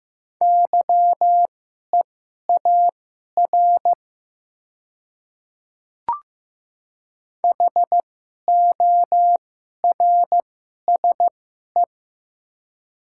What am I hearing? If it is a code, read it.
Morse code, heard as a keyed tone, about 15 wpm, spaced as for 11 wpm, 700 Hz. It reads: YEAR HORSE
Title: 2x Character Spacing - Top 300 Words - 15wpm